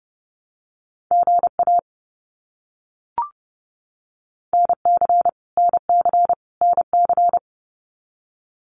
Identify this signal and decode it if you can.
Morse code, heard as a keyed tone, about 30 wpm, 700 Hz. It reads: GA NC NC NC
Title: US States Repeated 3x 30wpm